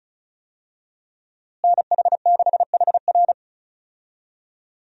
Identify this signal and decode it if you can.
Morse code, heard as a keyed tone, about 34 wpm, 700 Hz. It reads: NH6HR
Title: USA Call Signs - 5 Characters - Repeated 3x - 35wpm